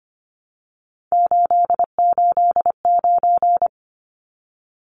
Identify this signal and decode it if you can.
Morse code, heard as a keyed tone, about 24 wpm, 690 Hz. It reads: 889
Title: Numbers - 3-Digit 25wpm